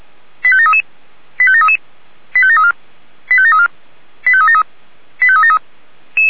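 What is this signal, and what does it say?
Five-tone selective call (ZVEI), heard as quick runs of five tones, tones of about 70 ms. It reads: 85733 85733 85734 85734 85373 85373
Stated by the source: Five-tone ZVEI selective call received on an SDR.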